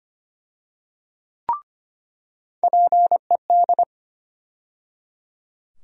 Morse code, heard as a keyed tone, about 25 wpm, 700 Hz. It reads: PED
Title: USA Call Sign Postfixes - 3 Letters 25wpm